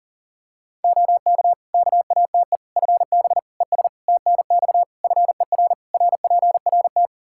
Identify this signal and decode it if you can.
Morse code, heard as a keyed tone, about 40 wpm, 700 Hz. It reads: OK KATE FB ES TNX FER RPRT
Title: Practice QSO - Exchange 2 - Info - 40wpm